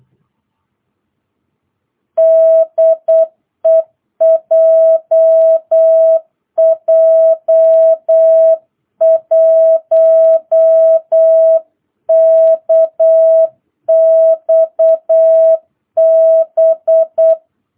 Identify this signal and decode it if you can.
Morse code, heard as a keyed tone, about 8 wpm, 660 Hz. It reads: DE JJ1KXB